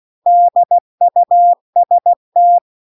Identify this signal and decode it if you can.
Morse code, heard as a keyed tone, about 16 wpm, 700 Hz. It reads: DUST